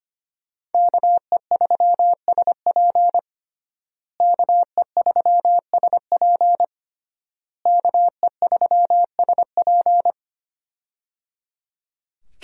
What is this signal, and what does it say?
Morse code, heard as a keyed tone, about 25 wpm, 700 Hz. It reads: KE3SP KE3SP KE3SP
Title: USA Call Signs - 5 Characters - Repeated 3x - 25wpm